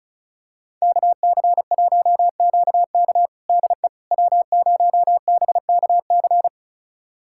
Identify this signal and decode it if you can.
Morse code, heard as a keyed tone, about 34 wpm, 690 Hz. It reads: KC1QK DE W0BKC
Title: USA Callsign DE Callsign - 5 Characters - 35wpm